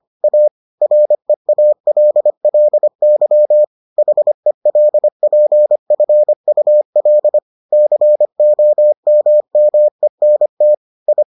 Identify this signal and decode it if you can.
Morse code, heard as a keyed tone, about 25 wpm, 600 Hz. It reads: A REALLY HELPFUL COMMENT I